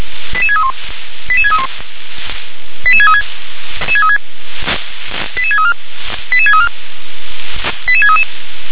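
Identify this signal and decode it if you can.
Five-tone selective call (ZVEI), heard as quick runs of five tones, tones of about 70 ms. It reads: 88631 88631 88636 88634 88634 88633
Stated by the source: ZVEI five-tone selective call received on SDR.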